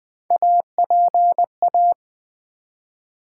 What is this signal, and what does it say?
Morse code, heard as a keyed tone, about 20 wpm, 700 Hz. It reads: APA
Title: Build A Word - 1 Letter At A Time - 5-Letter Words - 20wpm